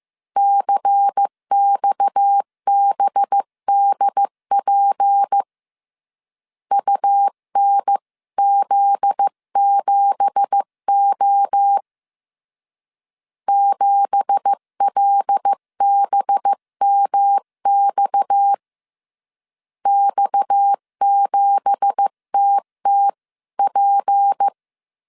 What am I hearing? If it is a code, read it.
Morse code, heard as a keyed tone, about 15 wpm, 800 Hz. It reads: CXBDP UNZ7O 7LBMX X7TTP
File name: CW / DET / 1 9 1500C S B 14wpm 3302